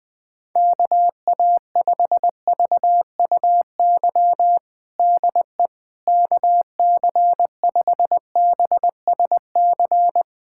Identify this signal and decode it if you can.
Morse code, heard as a keyed tone, about 20 wpm, 700 Hz. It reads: KA5VUY DE KC5BSC
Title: USA Callsign DE Callsign - 6 Characters - 20wpm